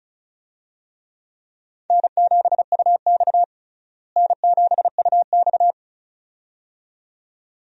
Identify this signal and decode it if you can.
Morse code, heard as a keyed tone, about 36 wpm, 700 Hz. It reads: N7UX N7UX
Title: USA Call Signs - 4 Characters - Repeated 2x - 35wpm